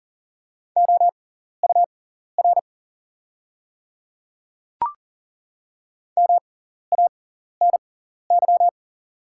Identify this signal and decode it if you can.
Morse code, heard as a keyed tone, about 40 wpm, spaced as for 14 wpm, 700 Hz. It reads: OUR MANY
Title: 6x Character Spacing - Top 200 Words - 40wpm